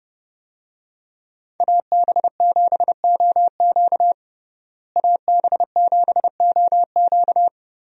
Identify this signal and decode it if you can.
Morse code, heard as a keyed tone, about 30 wpm, 700 Hz. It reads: AB7OQ AB7OQ
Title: USA Call Signs - 5 Characters - Repeated 3x - 30wpm